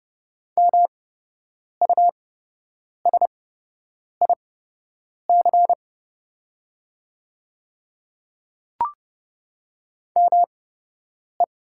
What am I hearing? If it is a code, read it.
Morse code, heard as a keyed tone, about 30 wpm, spaced as for 8 wpm, 700 Hz. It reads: MUSIC ME
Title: 8x Character Spacing - Top 300 Words - 30wpm